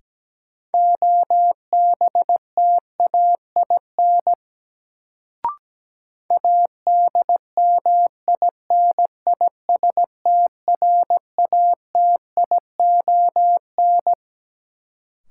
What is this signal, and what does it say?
Morse code, heard as a keyed tone, about 17 wpm, 700 Hz. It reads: OBTAIN ADMINISTRATION
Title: Sentences from Top 1300 Words - New Word Drill Alt - 17wpm